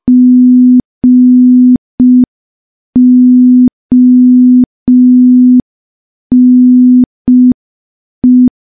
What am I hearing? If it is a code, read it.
Morse code, heard as a keyed tone, about 5 wpm, 250 Hz. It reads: GONE
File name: morse / morse-training-05WPM